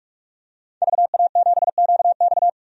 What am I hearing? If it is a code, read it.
Morse code, heard as a keyed tone, about 47 wpm, 700 Hz. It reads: VA7QX
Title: DXCC-100 Call Signs - 5 Characters - 45wpm